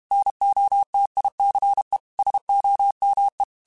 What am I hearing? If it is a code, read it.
Morse code, heard as a keyed tone, about 32 wpm, 800 Hz. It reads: NOTICE SOME